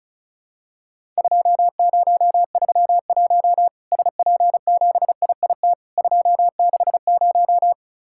Morse code, heard as a keyed tone, about 34 wpm, 700 Hz. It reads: 2031 SP7IIT 260
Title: Worked All Europe DX Contest - QTC Exchange - 35wpm